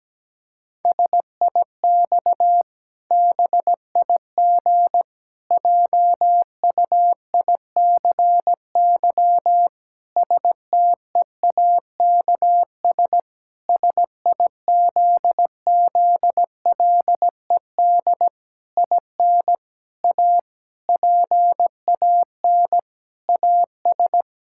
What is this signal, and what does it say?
Morse code, heard as a keyed tone, about 17 wpm, 700 Hz. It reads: SIX BIG JUICY STEAKS SIZZLED IN A PAN AS
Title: Pangram Challenge v2 - 17wpm